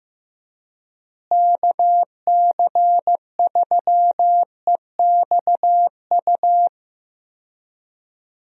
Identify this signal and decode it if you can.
Morse code, heard as a keyed tone, about 15 wpm, 700 Hz. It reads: KC3EXU
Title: USA Call Signs - 6 Characters - Repeated 2x - 15wpm